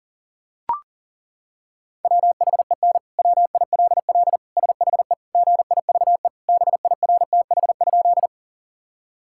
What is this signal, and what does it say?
Morse code, heard as a keyed tone, about 40 wpm, 700 Hz. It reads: WHEN WILL SHE GIVE BIRTH?
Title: Sentences from Top 1000 Words - Review Part 5 - 40wpm